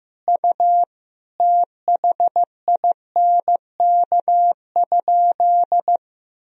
Morse code, heard as a keyed tone, about 15 wpm, 700 Hz. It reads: U THINK?
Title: Sentences from Top 200 Words and Abbreviations 15wpm